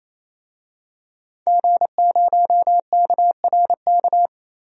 Morse code, heard as a keyed tone, about 28 wpm, 700 Hz. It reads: G0KRK
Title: DXCC-100 Call Signs - 5 Characters - Repeated 3x - 28wpm